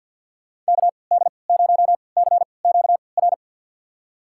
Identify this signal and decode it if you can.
Morse code, heard as a keyed tone, about 49 wpm, spaced as for 28 wpm, 700 Hz. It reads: KD0CQR
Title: USA Call Signs - 6 Characters - 3x Character Spacing - 50wpm